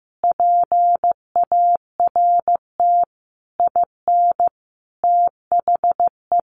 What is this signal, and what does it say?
Morse code, heard as a keyed tone, about 15 wpm, 700 Hz. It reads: PART IN THE
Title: Sentences from Top 1200 Words 15wpm